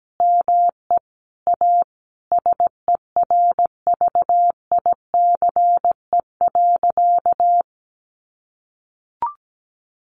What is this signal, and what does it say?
Morse code, heard as a keyed tone, about 17 wpm, 700 Hz. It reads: ME A SERVICE.
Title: Sentences from Top 700 Words 17wpm